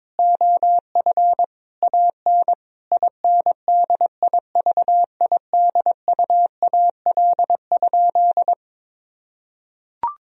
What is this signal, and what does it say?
Morse code, heard as a keyed tone, about 22 wpm, 700 Hz. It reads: OF AN INDIVIDUAL?
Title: Sentences from Top 1000 Words - Review Part 4 - 22wpm